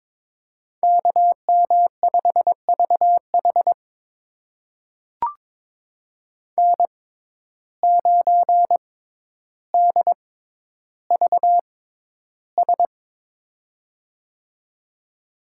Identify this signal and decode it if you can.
Morse code, heard as a keyed tone, about 22 wpm, spaced as for 11 wpm, 700 Hz. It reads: KM5VH N9DVS
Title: USA Call Signs - 5 Characters - 6x Character Spacing - 22wpm